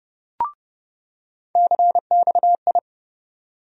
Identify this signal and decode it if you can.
Morse code, heard as a keyed tone, about 30 wpm, 700 Hz. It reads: CXI
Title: USA Call Sign Postfixes - 3 Letters 30wpm